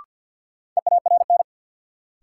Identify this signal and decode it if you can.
Morse code, heard as a keyed tone, about 52 wpm, 700 Hz. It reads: EARN
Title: Sentences from Top 1500 Words - New Word Drill - 50wpm